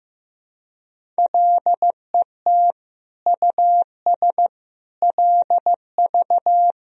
Morse code, heard as a keyed tone, about 15 wpm, 700 Hz. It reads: LET US LV
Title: Sentences from Top 400 Words and Abbreviations 15wpm